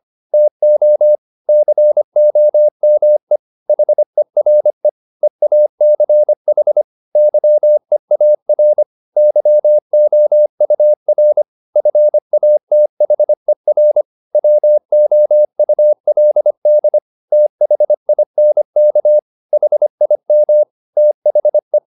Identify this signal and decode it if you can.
Morse code, heard as a keyed tone, about 25 wpm, 600 Hz. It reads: TO COME HERE EACH YEAR YOUR FATHER WOULD THINK HIM THE
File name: the old man and the sea part 1 25wpm